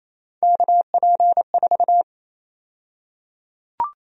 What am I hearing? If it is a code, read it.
Morse code, heard as a keyed tone, about 28 wpm, 700 Hz. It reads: KP4
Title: USA Call Sign Prefixes - 28wpm